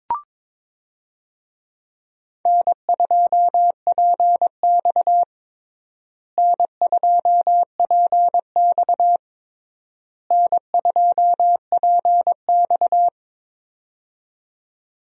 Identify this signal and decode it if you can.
Morse code, heard as a keyed tone, about 22 wpm, 700 Hz. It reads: N2PX N2PX N2PX
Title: USA Call Signs - 4 Characters - Repeated 3x - 22wpm